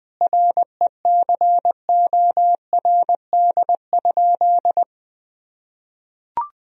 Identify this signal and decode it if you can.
Morse code, heard as a keyed tone, about 20 wpm, 700 Hz. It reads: RECORD?
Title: Sentences from Top 1000 Words - Review Part 3 - 20wpm